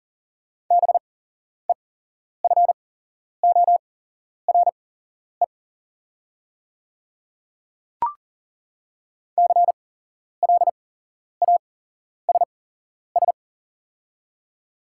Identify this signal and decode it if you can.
Morse code, heard as a keyed tone, about 40 wpm, spaced as for 11 wpm, 700 Hz. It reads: BEFORE CLASS
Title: 8x Character Spacing - Top 500 Words - 40wpm